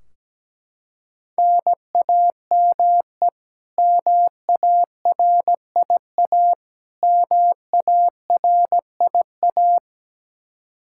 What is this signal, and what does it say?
Morse code, heard as a keyed tone, about 17 wpm, 700 Hz. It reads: NAME MARIA MARIA
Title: Practice QSO - Exchange 1 - Beginning - 17wpm